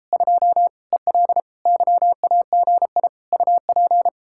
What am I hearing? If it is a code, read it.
Morse code, heard as a keyed tone, about 37 wpm, 700 Hz. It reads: 2 EL YAGI UP